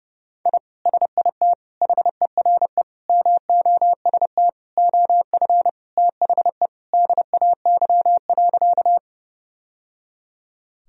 Morse code, heard as a keyed tone, about 30 wpm, 700 Hz. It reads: I SIT HERE MOST OF THE DAY.
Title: Sentences from Top 1000 Words - Review Part 1 - 30wpm